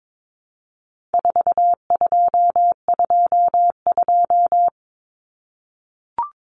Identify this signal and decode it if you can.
Morse code, heard as a keyed tone, about 22 wpm, 700 Hz. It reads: 4222
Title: Common Cut Numbers - 4-Digit 22wpm